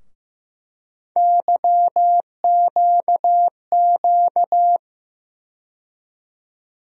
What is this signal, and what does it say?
Morse code, heard as a keyed tone, about 15 wpm, 700 Hz. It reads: YQQ
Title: USA Call Sign Postfixes - 3 Letters 15wpm